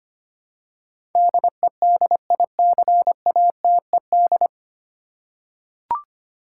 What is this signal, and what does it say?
Morse code, heard as a keyed tone, about 25 wpm, 700 Hz. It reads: DEDICATED
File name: practice / 25wpm / 9 Letter Words - First 3 Letters Given - 25wpm